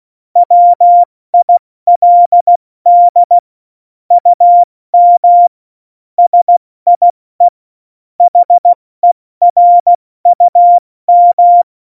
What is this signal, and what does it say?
Morse code, heard as a keyed tone, about 16 wpm, 700 Hz. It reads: WILD UM SIE HERUM